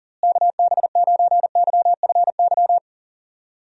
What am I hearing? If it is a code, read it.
Morse code, heard as a keyed tone, about 40 wpm, 700 Hz. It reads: KB9YFY
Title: USA Call Signs - 6 Characters - Repeated 3x - 40wpm